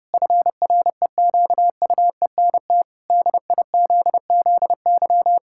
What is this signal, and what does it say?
Morse code, heard as a keyed tone, about 30 wpm, 700 Hz. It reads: FREQUENT DIZZY
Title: Pangram Challenge v2 - 30wpm